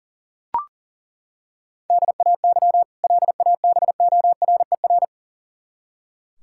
Morse code, heard as a keyed tone, about 40 wpm, 700 Hz. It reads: DAY LABORER
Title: K1USN SST QSO Practice 40wpm